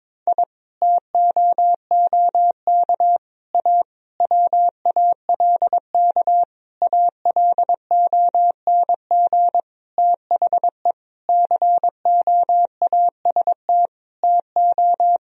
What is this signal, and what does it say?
Morse code, heard as a keyed tone, about 22 wpm, 700 Hz. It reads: I TOOK A WALK ALONG THE COAST TO
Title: Sentences from Top 1000 Words - Review Part 4 - 22wpm